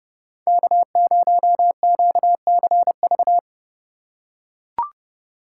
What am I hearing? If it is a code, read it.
Morse code, heard as a keyed tone, about 30 wpm, 700 Hz. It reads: K0QCV
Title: USA Call Signs - 2x Character Spacing - 30wpm